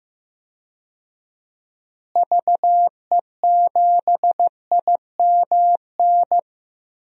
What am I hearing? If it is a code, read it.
Morse code, heard as a keyed tone, about 15 wpm, 700 Hz. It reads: VE7IMN
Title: DXCC-100 Call Signs - 6 Characters - Repeated 2x - 15wpm